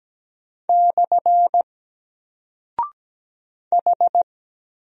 Morse code, heard as a keyed tone, about 17 wpm, 700 Hz. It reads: / H
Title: Single Character (Letter:Number:Punctuation:Prosign) Alt 17wpm